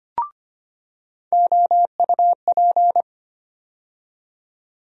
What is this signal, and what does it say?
Morse code, heard as a keyed tone, about 25 wpm, 700 Hz. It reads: OUP
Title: USA Call Sign Postfixes - 3 Letters 25wpm